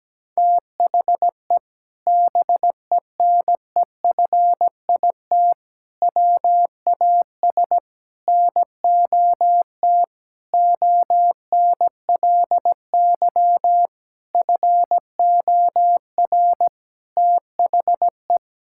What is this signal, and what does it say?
Morse code, heard as a keyed tone, about 17 wpm, 700 Hz. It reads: THE BENEFIT WAS NOT ONLY FOR THE
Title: Sentences from Top 1000 Words - Review Part 6 - 17wpm